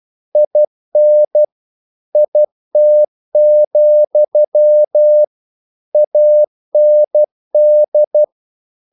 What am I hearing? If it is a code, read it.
Morse code, heard as a keyed tone, about 12 wpm, 600 Hz. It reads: IN IT, AND
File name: CW / Practice / If12